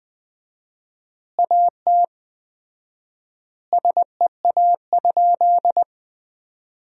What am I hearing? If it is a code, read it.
Morse code, heard as a keyed tone, about 21 wpm, 700 Hz. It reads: AT SEA?
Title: 4x Word-Spacing - Sentences from Top 300 Words - 20wpm